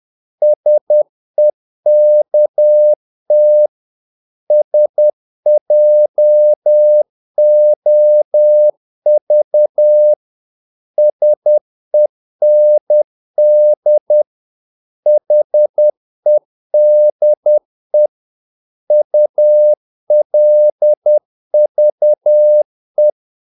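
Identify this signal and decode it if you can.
Morse code, heard as a keyed tone, about 10 wpm, 600 Hz. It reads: SEKT SJOV SEND HEDE ULVE
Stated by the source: Morse practice of short Danish words at 10 wpm, from CW med Gnister.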